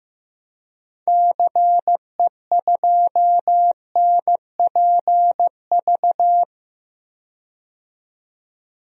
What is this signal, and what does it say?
Morse code, heard as a keyed tone, about 15 wpm, 700 Hz. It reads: CE2NPV
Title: DXCC-100 Call Signs - 6 Characters - 15wpm